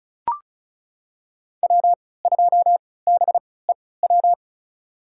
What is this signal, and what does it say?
Morse code, heard as a keyed tone, about 36 wpm, spaced as for 20 wpm, 700 Hz. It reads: W2BEW
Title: USA Call Signs - 5 Characters - 3x Character Spacing - 35wpm